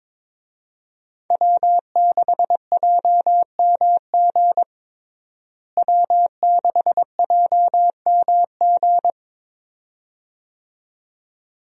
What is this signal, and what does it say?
Morse code, heard as a keyed tone, about 22 wpm, 700 Hz. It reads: W6JMG W6JMG
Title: USA Call Signs - 5 Characters - Repeated 2x - 22wpm